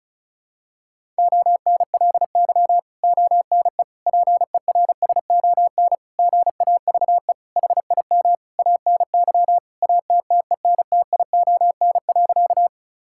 Morse code, heard as a keyed tone, about 36 wpm, 700 Hz. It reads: ONLY ONE PERSON GAVE HIM ANY ATTENTION.
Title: Sentences from Top 1000 Words - Review Part 6 - 35wpm